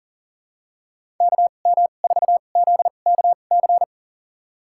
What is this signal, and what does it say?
Morse code, heard as a keyed tone, about 40 wpm, spaced as for 29 wpm, 700 Hz. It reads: KM4ZKC
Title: USA Call Signs - 2x Character Spacing - 40wpm